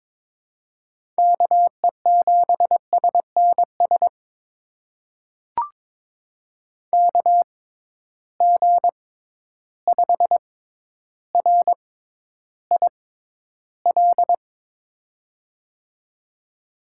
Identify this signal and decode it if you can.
Morse code, heard as a keyed tone, about 22 wpm, spaced as for 11 wpm, 700 Hz. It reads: KE7SNS KG5RIL
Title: USA Call Signs - 6 Characters - 6x Character Spacing - 22wpm